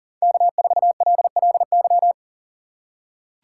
Morse code, heard as a keyed tone, about 38 wpm, 700 Hz. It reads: K4LLY
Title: USA Call Signs - 5 Characters - 2x Character Spacing - 40wpm